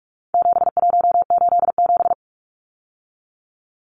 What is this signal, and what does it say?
Morse code, heard as a keyed tone, about 44 wpm, 700 Hz. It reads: 7187
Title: Numbers - 4-Digit 45wpm